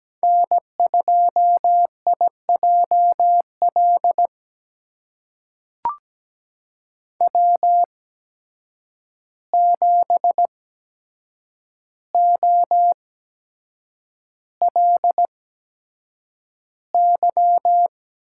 Morse code, heard as a keyed tone, about 17 wpm, spaced as for 7 wpm, 700 Hz. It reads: N2IJL W7OLY
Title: USA Call Signs - 5 Characters - 8x Character Spacing - 17wpm